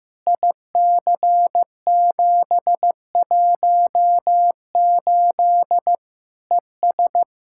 Morse code, heard as a keyed tone, about 15 wpm, 700 Hz. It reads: IC718 ES
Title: Practice QSO - Exchange 2 - Info - 15wpm